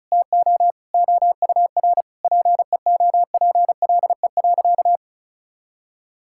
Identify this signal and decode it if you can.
Morse code, heard as a keyed tone, about 34 wpm, 690 Hz. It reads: TO OUR PEOPLE.
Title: Sentences from Top 1000 Words - Review Part 6 - 35wpm